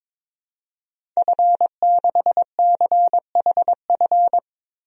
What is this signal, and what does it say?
Morse code, heard as a keyed tone, about 21 wpm, 700 Hz. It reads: F6CHF
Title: DXCC-100 Call Signs - 5 Characters - Repeated 3x - 22wpm